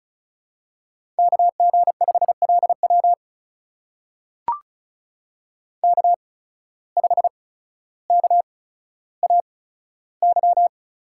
Morse code, heard as a keyed tone, about 36 wpm, spaced as for 15 wpm, 700 Hz. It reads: KG5LW K5KAY
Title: USA Call Signs - 5 Characters - 8x Character Spacing - 35wpm